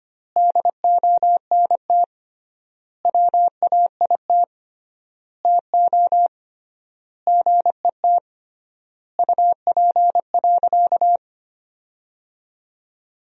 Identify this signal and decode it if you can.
Morse code, heard as a keyed tone, about 25 wpm, 700 Hz. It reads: DONT WAIT TO GET UP.
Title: 3x Word-Spacing - Sentences from Top 500 Words - 25wpm